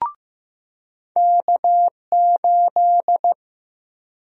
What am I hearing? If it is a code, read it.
Morse code, heard as a keyed tone, about 15 wpm, 700 Hz. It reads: K8
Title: USA Call Sign Prefixes - 15wpm